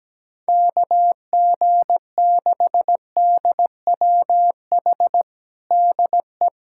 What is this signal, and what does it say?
Morse code, heard as a keyed tone, about 17 wpm, 700 Hz. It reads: KG6DWH DE
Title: USA Callsign DE Callsign - 6 Characters - 17wpm